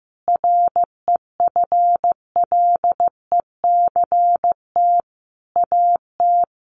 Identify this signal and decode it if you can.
Morse code, heard as a keyed tone, about 15 wpm, 700 Hz. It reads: REFLECT AT
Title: Sentences from Top 1500 Words 15wpm